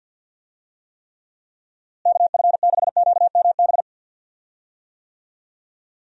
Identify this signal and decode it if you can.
Morse code, heard as a keyed tone, about 51 wpm, 700 Hz. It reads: KF6QMB
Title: USA Call Signs - 6 Characters - Repeated 3x - 50wpm